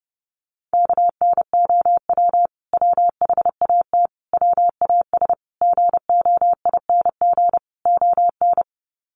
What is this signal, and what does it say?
Morse code, heard as a keyed tone, about 30 wpm, 700 Hz. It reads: KNOW WHAT WAS GOING ON
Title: Sets of 5 Words Encore2 30wpm